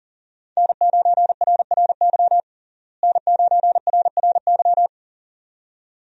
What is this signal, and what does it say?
Morse code, heard as a keyed tone, about 40 wpm, 700 Hz. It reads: N9RRY N9RRY
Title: USA Call Signs - 5 Characters - Repeated 2x - 40wpm